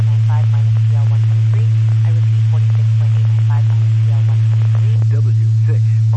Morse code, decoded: none

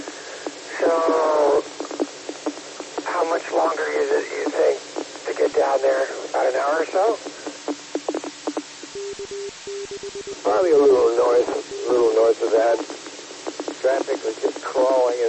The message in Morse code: none

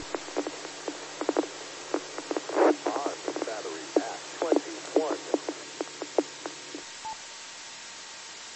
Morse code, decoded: none

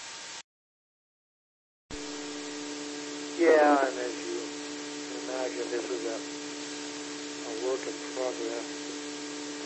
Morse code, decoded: none

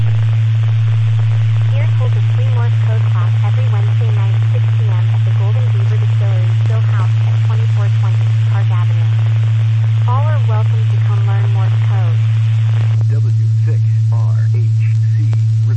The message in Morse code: none

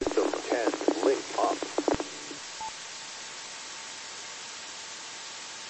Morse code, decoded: none

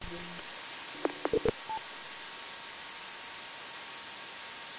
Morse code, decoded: none

